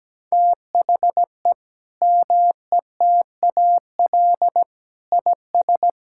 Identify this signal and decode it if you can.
Morse code, heard as a keyed tone, about 17 wpm, 700 Hz. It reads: THE METAL IS
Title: Sentences from Top 1000 Words - Review Part 6 - 17wpm